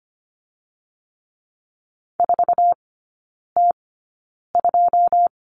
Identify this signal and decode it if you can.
Morse code, heard as a keyed tone, about 25 wpm, spaced as for 9 wpm, 690 Hz. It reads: 4T2
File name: Lesson 15e - Number 5 - Three Characters 25wpm